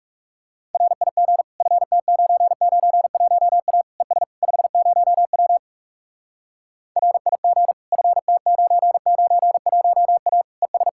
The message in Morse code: RIG FT991A ES 50W RIG FT991A ES

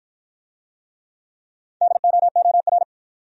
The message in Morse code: DOOR

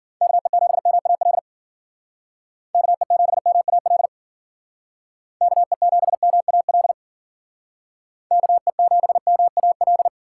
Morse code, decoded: KE7MAL KE7MAL KE7MAL KE7MAL

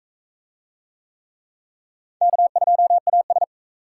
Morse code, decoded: K2AS

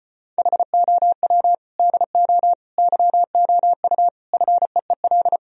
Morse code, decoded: HOW DO YOU FEEL

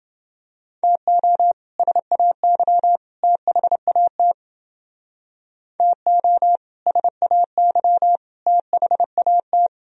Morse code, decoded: TO SAY THAT TO SAY THAT

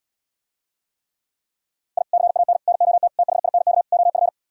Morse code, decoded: I CAN NOT THINK OF

